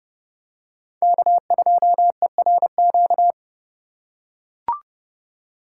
Morse code, heard as a keyed tone, about 32 wpm, 700 Hz. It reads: K2ERQ